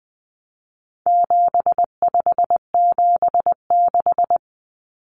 Morse code, heard as a keyed tone, about 20 wpm, 700 Hz. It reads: 7576
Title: Common Cut Numbers - 4-Digit 20wpm